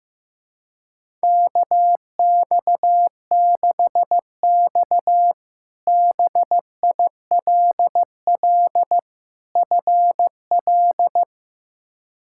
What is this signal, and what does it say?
Morse code, heard as a keyed tone, about 15 wpm, 700 Hz. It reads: KX6X BILL FL
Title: K1USN SST QSO Practice - 15wpm